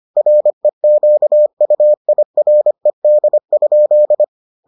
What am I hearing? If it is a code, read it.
Morse code, heard as a keyed tone, about 25 wpm, 600 Hz. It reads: REQUIRED?